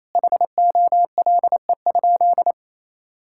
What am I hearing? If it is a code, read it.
Morse code, heard as a keyed tone, about 28 wpm, 700 Hz. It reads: HOLE?
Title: Sentences from Top 1600 Words 28wpm